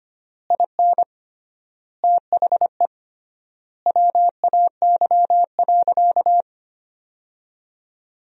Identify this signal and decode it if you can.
Morse code, heard as a keyed tone, about 25 wpm, 700 Hz. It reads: IN THE WAY.
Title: 3x Word-Spacing - Sentences from Top 400 Words - 25wpm